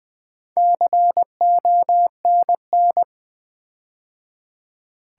Morse code, heard as a keyed tone, about 21 wpm, 700 Hz. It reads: CONN
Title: Build A Word - 4 Letters At A Time - 10-Letter Words - 20wpm